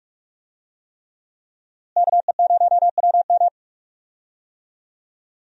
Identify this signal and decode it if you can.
Morse code, heard as a keyed tone, about 46 wpm, 700 Hz. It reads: KE0WM
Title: USA Call Signs - 5 Characters - Repeated 3x - 45wpm